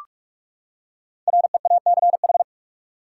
Morse code, heard as a keyed tone, about 46 wpm, 700 Hz. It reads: REACH
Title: Five Letter Words - 45wpm